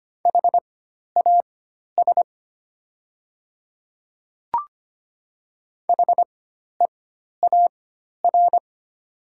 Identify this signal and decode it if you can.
Morse code, heard as a keyed tone, about 25 wpm, spaced as for 12 wpm, 700 Hz. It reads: HAS HEAR
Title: 4x Character Spacing - Top 300 Words - 25wpm